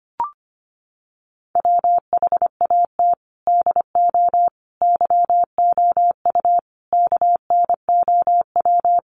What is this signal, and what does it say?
Morse code, heard as a keyed tone, about 25 wpm, 700 Hz. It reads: WHAT DO YOU KNOW
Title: Sentences from Top 100 Words 25wpm